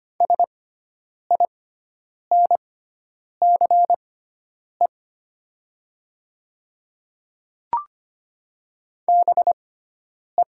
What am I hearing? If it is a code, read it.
Morse code, heard as a keyed tone, about 25 wpm, spaced as for 9 wpm, 700 Hz. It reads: SINCE BE
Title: 6x Character Spacing - Top 500 Words - 25wpm